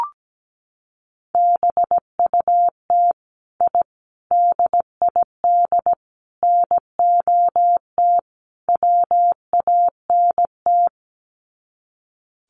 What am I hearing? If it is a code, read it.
Morse code, heard as a keyed tone, about 17 wpm, 700 Hz. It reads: BUT I DID NOT WANT
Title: Sets of 5 Words Encore 17wpm